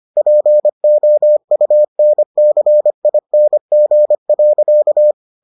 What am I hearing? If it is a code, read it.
Morse code, heard as a keyed tone, about 25 wpm, 600 Hz. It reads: POUNCING.